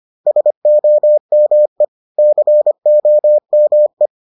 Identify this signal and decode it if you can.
Morse code, heard as a keyed tone, about 25 wpm, 600 Hz. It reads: SOME COME